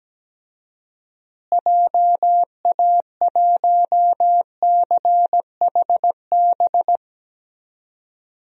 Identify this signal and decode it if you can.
Morse code, heard as a keyed tone, about 17 wpm, 700 Hz. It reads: JA1CHB
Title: DXCC-100 Call Signs - 6 Characters - Repeated 3x - 17wpm